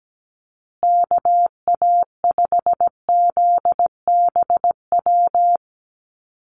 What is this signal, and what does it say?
Morse code, heard as a keyed tone, about 17 wpm, 700 Hz. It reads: KA5ZBW
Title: USA Call Signs Repeated 2x 17wpm